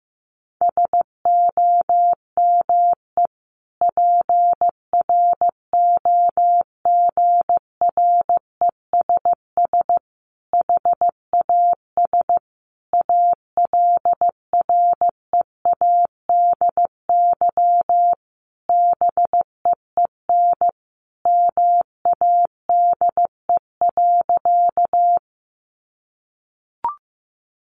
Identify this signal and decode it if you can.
Morse code, heard as a keyed tone, about 15 wpm, 700 Hz. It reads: SOME PROGRESS HAS ALREADY BEEN MADE.
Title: Sentences from Top 1400 Words 15wpm